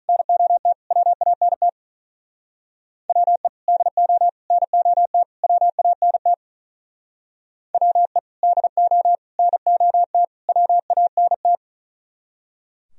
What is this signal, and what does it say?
Morse code, heard as a keyed tone, about 45 wpm, 700 Hz. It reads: NOT WANT WE DO NOT WANT WE DO NOT WANT